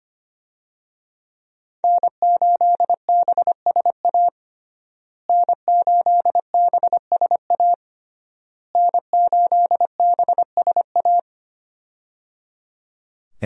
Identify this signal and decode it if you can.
Morse code, heard as a keyed tone, about 25 wpm, 700 Hz. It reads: N8BSA N8BSA N8BSA
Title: USA Call Signs - 5 Characters - Repeated 3x - 25wpm